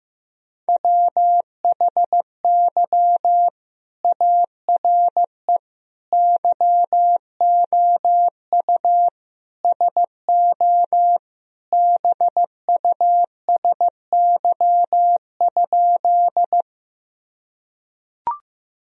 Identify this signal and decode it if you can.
Morse code, heard as a keyed tone, about 15 wpm, 700 Hz. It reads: WHY ARE YOU SO BUSY?
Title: Sentences from Top 1000 Words - Review Part 3 - 15wpm